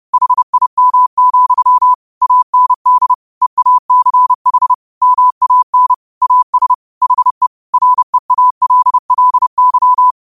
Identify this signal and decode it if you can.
Morse code, heard as a keyed tone, about 30 wpm, 1000 Hz. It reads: HIM, AND EACH MAN AS HE REALLY
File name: qotdfiles / 30wpm